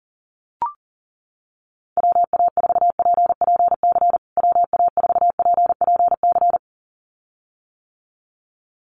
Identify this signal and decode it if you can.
Morse code, heard as a keyed tone, about 40 wpm, 700 Hz. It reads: WA4PPC WA4PPC
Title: USA Call Signs Repeated 2x 40wpm